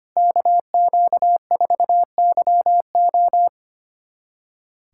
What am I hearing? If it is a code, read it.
Morse code, heard as a keyed tone, about 25 wpm, 700 Hz. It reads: KQ4YO
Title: USA Callsign DE Callsign - 5 Characters - 25wpm